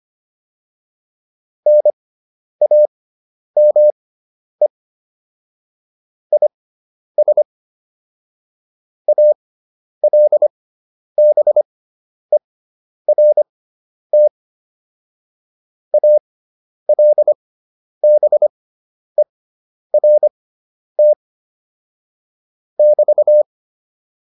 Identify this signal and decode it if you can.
Morse code, heard as a keyed tone, about 25 wpm, spaced as for 10 wpm, 600 Hz. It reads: NAME IS ALBERT ALBERT <BT>